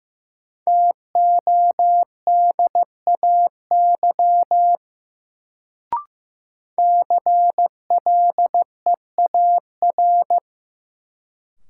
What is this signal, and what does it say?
Morse code, heard as a keyed tone, about 15 wpm, 700 Hz. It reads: TODAY CLEAR
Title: Sets of 5 Words Encore - New Word Drill - 15wpm